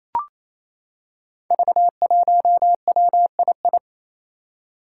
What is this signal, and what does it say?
Morse code, heard as a keyed tone, about 27 wpm, 700 Hz. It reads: V1WII
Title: Five Challenging Letter-Number Combinations 28wpm